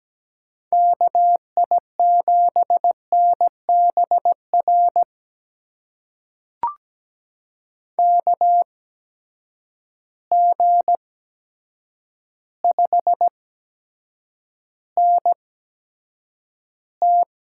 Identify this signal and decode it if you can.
Morse code, heard as a keyed tone, about 17 wpm, spaced as for 8 wpm, 700 Hz. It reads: KI7NBR KG5NT
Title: USA Call Signs - 8x Character Spacing - 17wpm